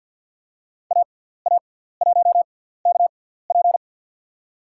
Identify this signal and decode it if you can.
Morse code, heard as a keyed tone, about 50 wpm, spaced as for 17 wpm, 690 Hz. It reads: AA1KP
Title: USA Call Signs - 5 Characters - 6x Character Spacing - 50wpm